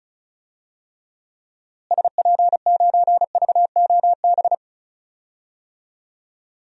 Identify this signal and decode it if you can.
Morse code, heard as a keyed tone, about 36 wpm, 700 Hz. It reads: SP9VOB
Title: DXCC-100 Call Signs - 6 Characters - Repeated 2x - 35wpm